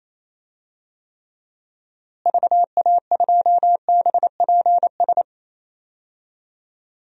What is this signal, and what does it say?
Morse code, heard as a keyed tone, about 28 wpm, 700 Hz. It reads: VA2BPS
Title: DXCC-100 Call Signs - 6 Characters - Repeated 2x - 28wpm